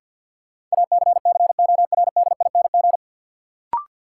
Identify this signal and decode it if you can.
Morse code, heard as a keyed tone, about 52 wpm, 700 Hz. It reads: ACCORDING
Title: Sets of 2 Words - New Word Drill - 50wpm